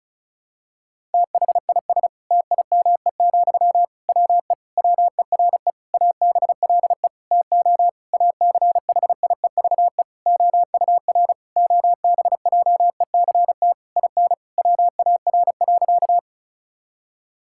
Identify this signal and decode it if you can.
Morse code, heard as a keyed tone, about 36 wpm, 700 Hz. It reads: THIS TIME, WE WERE ABLE TO ACHIEVE OUR OBJECT IN WAR.
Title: Sentences from Top 1000 Words - Review Part 6 - 35wpm